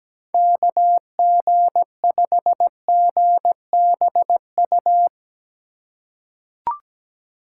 Morse code, heard as a keyed tone, about 17 wpm, 700 Hz. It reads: KG5GBU
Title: USA Call Signs - 6 Characters - 8x Character Spacing - 17wpm